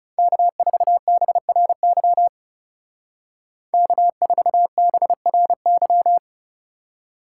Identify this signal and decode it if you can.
Morse code, heard as a keyed tone, about 40 wpm, 700 Hz. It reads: K4BRY K4BRY